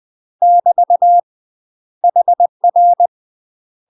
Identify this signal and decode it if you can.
Morse code, heard as a keyed tone, about 20 wpm, 700 Hz. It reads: <BT> HR
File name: UK MASTER QSOs TRAINING / 20WPM / UK MASTER QSOs 0060